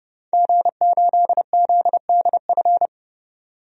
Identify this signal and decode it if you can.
Morse code, heard as a keyed tone, about 30 wpm, 700 Hz. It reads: G8ZDF
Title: DXCC-100 Call Signs - 5 Characters - 30wpm